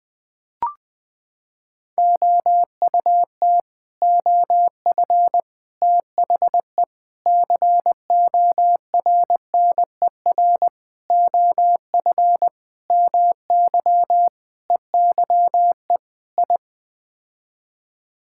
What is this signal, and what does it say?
Morse code, heard as a keyed tone, about 19 wpm, 700 Hz. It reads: OUT OF THE CORNER OF MY EYE I
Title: Sets of 8 Words - 20wpm